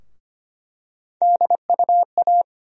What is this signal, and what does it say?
Morse code, heard as a keyed tone, about 25 wpm, 700 Hz. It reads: DUA
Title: USA Call Sign Postfixes - 3 Letters 25wpm